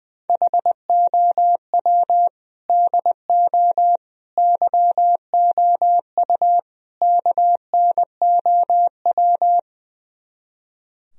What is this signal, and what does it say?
Morse code, heard as a keyed tone, about 20 wpm, 700 Hz. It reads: HOW DO YOU KNOW
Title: Speed-Racing - Sets of 4 Words 20wpm